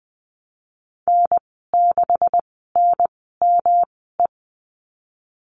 Morse code, heard as a keyed tone, about 20 wpm, spaced as for 15 wpm, 700 Hz. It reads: N6NME
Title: 128 Lesson 21g - Number 6 - Call Signs 20wpm